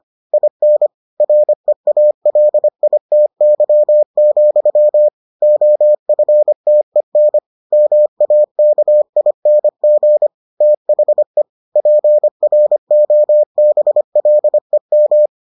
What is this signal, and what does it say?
Morse code, heard as a keyed tone, about 25 wpm, 600 Hz. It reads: IN REALITY, OFTEN MAKING THE PROBLEM